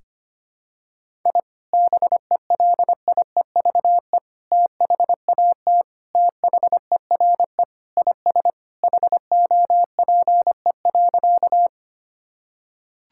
Morse code, heard as a keyed tone, about 25 wpm, 700 Hz. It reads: I BELIEVE THAT THERE IS HOPE.
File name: practice / 25wpm / Sentences with 3-Word Phrases - 25wpm